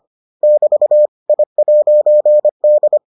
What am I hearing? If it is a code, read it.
Morse code, heard as a keyed tone, about 25 wpm, 600 Hz. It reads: <BT> I'D